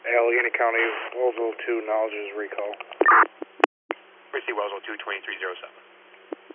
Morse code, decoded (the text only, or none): none